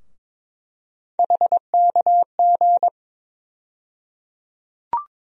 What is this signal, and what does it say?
Morse code, heard as a keyed tone, about 22 wpm, 700 Hz. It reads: HKG
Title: USA Call Sign Postfixes - 3 Letters 22wpm